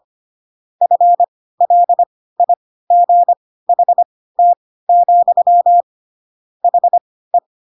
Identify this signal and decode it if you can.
Morse code, heard as a keyed tone, about 25 wpm, spaced as for 16 wpm, 700 Hz. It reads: FLIGHT, HE
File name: Pinocchio / Pinocchio ch4 10 25 ik1hge